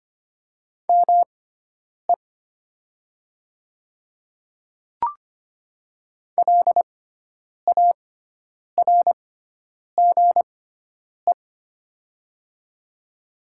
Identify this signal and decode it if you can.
Morse code, heard as a keyed tone, about 25 wpm, spaced as for 9 wpm, 700 Hz. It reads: ME LARGE
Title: 6x Character Spacing - Top 500 Words - 25wpm